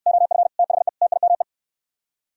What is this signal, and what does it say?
Morse code, heard as a keyed tone, about 64 wpm, 700 Hz. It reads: OF THE TIME